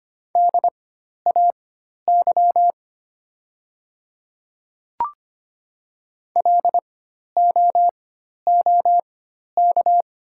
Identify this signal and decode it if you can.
Morse code, heard as a keyed tone, about 25 wpm, spaced as for 12 wpm, 700 Hz. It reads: DAY LOOK
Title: 4x Character Spacing - Top 100 Words - 25wpm